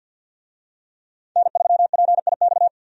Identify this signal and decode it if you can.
Morse code, heard as a keyed tone, about 51 wpm, 700 Hz. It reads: N3PIX
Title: USA Call Signs - 5 Characters - 50wpm